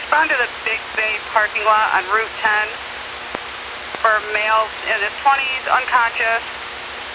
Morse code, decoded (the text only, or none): none